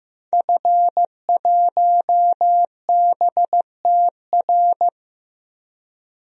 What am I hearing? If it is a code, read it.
Morse code, heard as a keyed tone, about 15 wpm, 700 Hz. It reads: F1BTR
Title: DXCC-100 Call Signs - 5 Characters - Repeated 3x - 15wpm